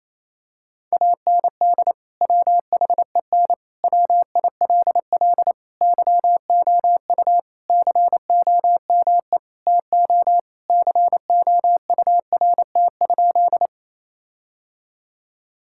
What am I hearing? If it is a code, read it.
Morse code, heard as a keyed tone, about 28 wpm, 700 Hz. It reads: AND WHEN WILL YOU COME TO COURT?
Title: Sentences from Top 1600 Words 28wpm